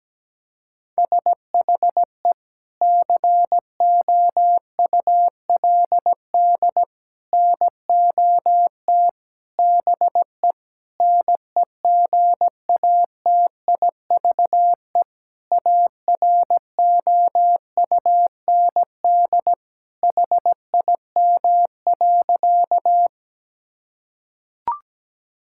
Morse code, heard as a keyed tone, about 17 wpm, 700 Hz. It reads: SHE COULD NOT BE NEGATIVE AROUND HIM.
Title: Sentences from Top 1000 Words - Review Part 6 - 17wpm